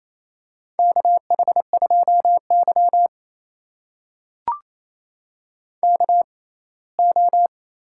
Morse code, heard as a keyed tone, about 28 wpm, spaced as for 19 wpm, 700 Hz. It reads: KH2Y KO
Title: USA Call Signs - 4 Characters - 6x Character Spacing - 28wpm